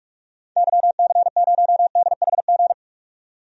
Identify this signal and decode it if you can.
Morse code, heard as a keyed tone, about 45 wpm, 700 Hz. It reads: YC0DHG
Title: DXCC-100 Call Signs - 6 Characters - 45wpm